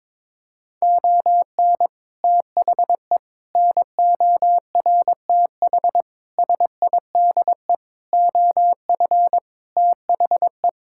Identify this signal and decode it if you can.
Morse code, heard as a keyed tone, about 22 wpm, 700 Hz. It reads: ON THE NORTH SIDE OF THE
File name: practice / 22wpm / Sets of 6 Words - 22wpm